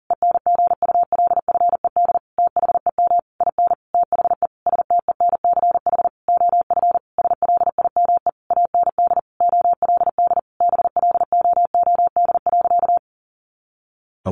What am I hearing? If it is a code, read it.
Morse code, heard as a keyed tone, about 37 wpm, 700 Hz. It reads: ENGULFED THEM IN THE STENCH OF SLIME AND OLD BLOOD.
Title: A Sound Of Thunder 40wpm